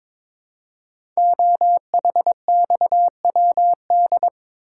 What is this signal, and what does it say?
Morse code, heard as a keyed tone, about 22 wpm, 700 Hz. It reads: OHXWD
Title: Code Groups - 5 Letters - 22wpm